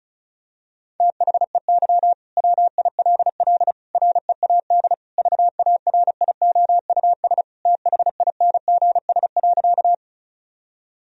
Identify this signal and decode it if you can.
Morse code, heard as a keyed tone, about 36 wpm, 700 Hz. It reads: THEY WILL READ VARIOUS THINGS.
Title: Sentences from Top 1000 Words - Review Part 4 - 35wpm